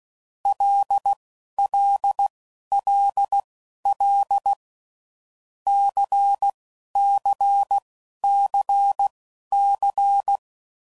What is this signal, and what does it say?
Morse code, heard as a keyed tone, about 16 wpm, spaced as for 11 wpm, 800 Hz. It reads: LLLL CCCC
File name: Groups / Basic Letter Groups / LFCYQ